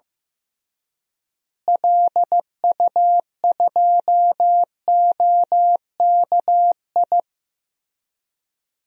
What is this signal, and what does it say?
Morse code, heard as a keyed tone, about 14 wpm, 700 Hz. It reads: LU2OKI
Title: DXCC-100 Call Signs - 6 Characters - Repeated 3x - 15wpm